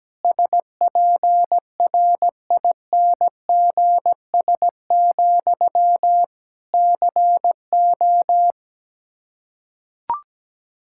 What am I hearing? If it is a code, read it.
Morse code, heard as a keyed tone, about 17 wpm, 700 Hz. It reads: SPRINGS, CO
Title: Practice QSO - Exchange 1 - Beginning - 17wpm